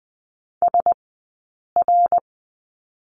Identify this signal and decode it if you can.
Morse code, heard as a keyed tone, about 20 wpm, 700 Hz. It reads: S R
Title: 059 Lesson 11d - Letter h - Two Characters 20wpm